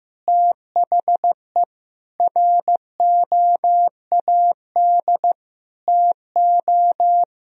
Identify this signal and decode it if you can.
Morse code, heard as a keyed tone, about 15 wpm, 700 Hz. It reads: THE ROAD TO
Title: Sentences from Top 1000 Words - Review Part 2 - 15wpm